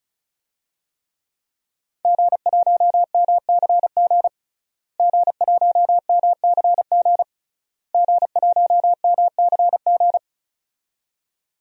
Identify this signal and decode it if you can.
Morse code, heard as a keyed tone, about 34 wpm, 700 Hz. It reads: G1MCG G1MCG G1MCG
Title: DXCC-100 Call Signs - 5 Characters - Repeated 3x - 35wpm